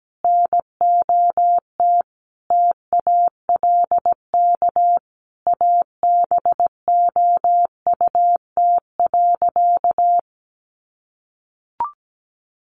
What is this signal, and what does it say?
Morse code, heard as a keyed tone, about 17 wpm, 700 Hz. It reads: NOT TALK ABOUT.
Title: Sentences from Top 800 Words 17wpm